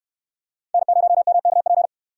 Morse code, heard as a keyed tone, about 68 wpm, 700 Hz. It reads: N0MKP